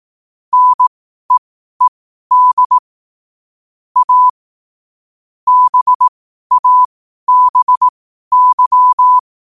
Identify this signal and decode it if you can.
Morse code, heard as a keyed tone, about 18 wpm, spaced as for 12 wpm, 1000 Hz. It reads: NEED A BABY
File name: qotdfiles / 12wpm